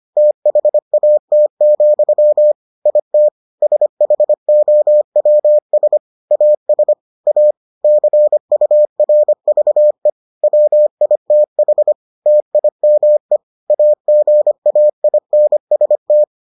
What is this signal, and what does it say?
Morse code, heard as a keyed tone, about 25 wpm, 600 Hz. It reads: THAT, IT SHOWS AS A CURVE WITH TIME AGAINST